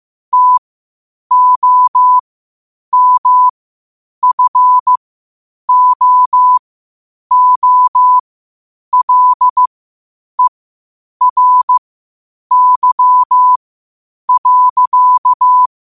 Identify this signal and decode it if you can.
Morse code, heard as a keyed tone, about 15 wpm, spaced as for 8 wpm, 1000 Hz. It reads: TOMFOOLERY.
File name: qotdfiles / 8wpm